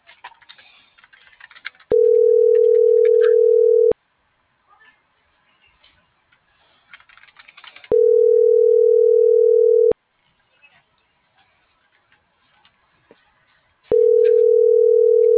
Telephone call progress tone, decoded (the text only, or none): ringback tone